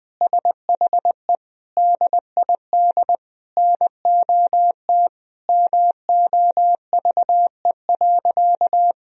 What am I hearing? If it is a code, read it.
Morse code, heard as a keyed tone, about 20 wpm, 700 Hz. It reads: SHE DID NOT MOVE.
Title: Sentences from Top 1000 Words - Review Part 1 - 20wpm